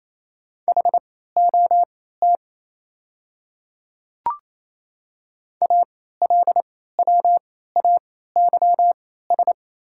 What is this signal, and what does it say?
Morse code, heard as a keyed tone, about 28 wpm, spaced as for 16 wpm, 690 Hz. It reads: HOT ALWAYS
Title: 3x Character Spacing - Top 500 Words - 28wpm